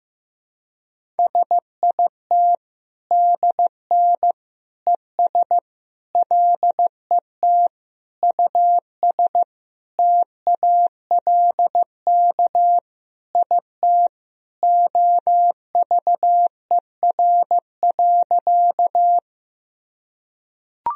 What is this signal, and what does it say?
Morse code, heard as a keyed tone, about 15 wpm, 700 Hz. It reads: SIT DN ES LET US TALK IT OVER.
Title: Sentences from Top 500 Words and Abbreviations 15wpm